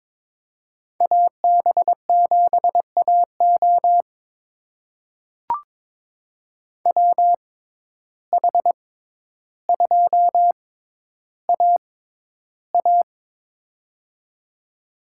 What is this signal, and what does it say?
Morse code, heard as a keyed tone, about 22 wpm, spaced as for 11 wpm, 700 Hz. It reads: AB7AO WH2AA
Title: USA Call Signs - 5 Characters - 6x Character Spacing - 22wpm